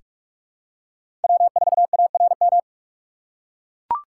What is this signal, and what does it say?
Morse code, heard as a keyed tone, about 41 wpm, 700 Hz. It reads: W4ARM
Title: USA Call Signs - 5 Characters - 2x Character Spacing - 45wpm